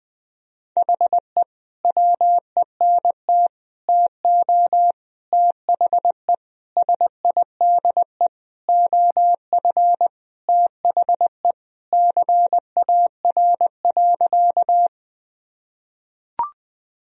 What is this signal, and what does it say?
Morse code, heard as a keyed tone, about 21 wpm, 700 Hz. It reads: HE WENT TO THE SIDE OF THE CAR.
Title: Sentences from Top 1000 Words - Review Part 1 - 20wpm